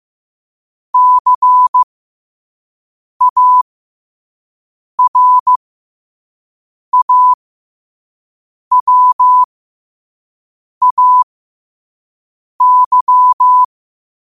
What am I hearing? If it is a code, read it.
Morse code, heard as a keyed tone, about 15 wpm, spaced as for 5 wpm, 1000 Hz. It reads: CARAWAY
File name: qotdfiles / 5wpm